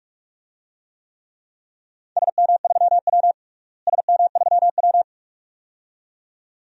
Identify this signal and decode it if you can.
Morse code, heard as a keyed tone, about 41 wpm, 700 Hz. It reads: SM3W SM3W
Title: DXCC-100 Call Signs - 4 Characters - Repeated 2x - 45wpm